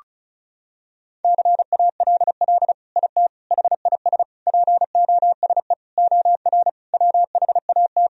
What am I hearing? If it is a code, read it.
Morse code, heard as a keyed tone, about 34 wpm, 700 Hz. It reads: CALL IT HIS POSE OR WHAT
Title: Sentences from Top 1000 Words - Review Part 2 - 35wpm